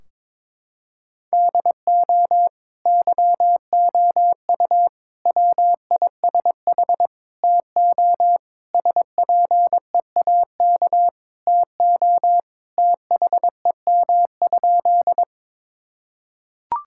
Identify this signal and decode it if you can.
Morse code, heard as a keyed tone, about 22 wpm, 700 Hz. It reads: DO YOU WISH TO SPEAK TO THEM?
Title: Sentences from Top 1600 Words 22wpm